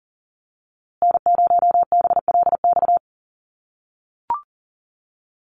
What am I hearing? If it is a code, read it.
Morse code, heard as a keyed tone, about 40 wpm, 700 Hz. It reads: N0BLX
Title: USA Call Signs Repeated 2x 40wpm